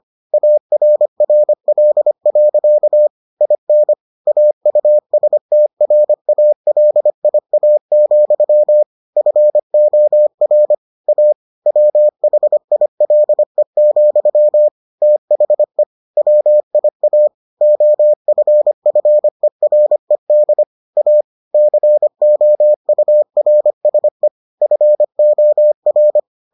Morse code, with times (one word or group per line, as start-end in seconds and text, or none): 0.3-3.1 ARRL.
3.4-3.9 IN
4.3-8.8 AUSTRALIA,
9.2-10.7 FOR
11.1-11.3 A
11.7-14.7 WHILE,
15.0-15.8 THE
16.2-17.3 WIA
17.6-20.6 OFFERED
21.0-21.2 A
21.5-24.3 COURSE
24.6-26.2 FOR